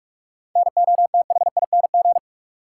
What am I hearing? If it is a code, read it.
Morse code, heard as a keyed tone, about 46 wpm, 700 Hz. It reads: NOTHING
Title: Sets of 4 Words - New Word Drill - 45wpm